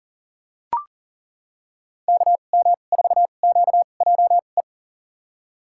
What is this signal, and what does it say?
Morse code, heard as a keyed tone, about 38 wpm, spaced as for 28 wpm, 690 Hz. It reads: KM4QJE
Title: USA Call Signs - 6 Characters - 2x Character Spacing - 40wpm